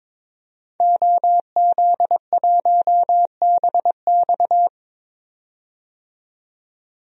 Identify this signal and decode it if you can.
Morse code, heard as a keyed tone, about 22 wpm, 700 Hz. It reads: OZ1BX
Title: DXCC-100 Call Signs - 5 Characters - 22wpm